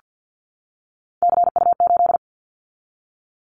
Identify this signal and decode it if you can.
Morse code, heard as a keyed tone, about 50 wpm, 700 Hz. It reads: CU8
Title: Callsign Prefixes for Top 100 DXCC - 50wpm